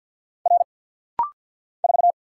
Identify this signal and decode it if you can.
Morse code, heard as a keyed tone, about 48 wpm, 700 Hz. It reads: R 4
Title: Single Letter-Number - Rapid-Fire 50wpm